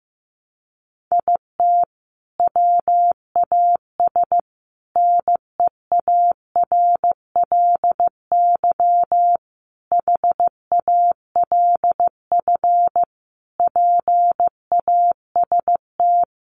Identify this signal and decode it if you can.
Morse code, heard as a keyed tone, about 15 wpm, 700 Hz. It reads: IT WAS NEARLY HALF PAST
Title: Sentences from Top 1300 Words 15wpm